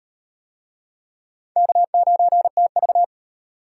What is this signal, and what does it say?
Morse code, heard as a keyed tone, about 37 wpm, 700 Hz. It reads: K9TV